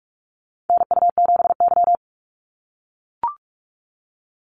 Morse code, heard as a keyed tone, about 44 wpm, 700 Hz. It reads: NU7Y
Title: Call Signs - 45wpm